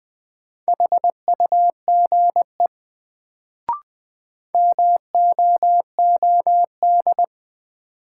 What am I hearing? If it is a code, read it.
Morse code, heard as a keyed tone, about 20 wpm, 700 Hz. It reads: HUGE MOOD
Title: Sentences from Top 1100 Words - New Word Drill Alt - 20wpm